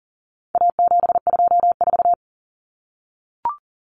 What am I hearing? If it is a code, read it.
Morse code, heard as a keyed tone, about 40 wpm, 700 Hz. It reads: A724
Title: Common Cut Numbers - 4-Digit 40wpm